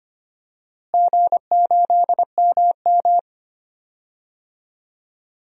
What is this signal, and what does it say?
Morse code, heard as a keyed tone, about 25 wpm, 700 Hz. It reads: G8MM
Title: DXCC-100 Call Signs - 4 Characters - Repeated 2x - 25wpm